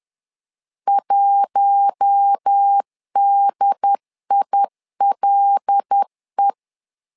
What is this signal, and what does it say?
Morse code, heard as a keyed tone, about 10 wpm, 800 Hz. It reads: 1DILE
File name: LEZIONI / 5 LBVC / 5 L 10wpm 2259